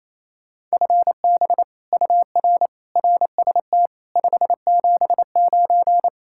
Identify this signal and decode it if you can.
Morse code, heard as a keyed tone, about 34 wpm, 700 Hz. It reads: FB UR RST 579
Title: Speed-Racing Common QSO Phrases - 28wpm